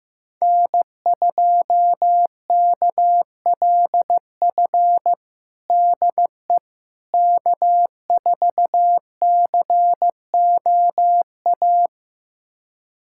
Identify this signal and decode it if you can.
Morse code, heard as a keyed tone, about 15 wpm, 700 Hz. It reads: N2KLF DE K4COA
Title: USA Callsign DE Callsign - 5 Characters - 15wpm